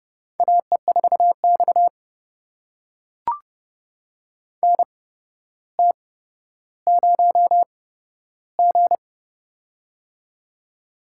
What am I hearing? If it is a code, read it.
Morse code, heard as a keyed tone, about 30 wpm, spaced as for 13 wpm, 700 Hz. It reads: AE4X NT0G
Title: USA Call Signs - 4 Characters - 8x Character Spacing - 30wpm